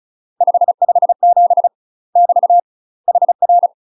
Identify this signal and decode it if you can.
Morse code, heard as a keyed tone, about 36 wpm, 700 Hz. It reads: 557 <BT> HR
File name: UK MASTER QSOs TRAINING / 35WPM / UK MASTER QSOs 0002